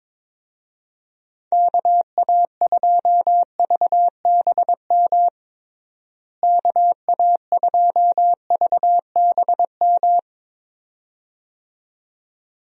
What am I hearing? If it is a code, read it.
Morse code, heard as a keyed tone, about 22 wpm, 700 Hz. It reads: KA2VBM KA2VBM
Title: USA Call Signs - 6 Characters - Repeated 2x - 22wpm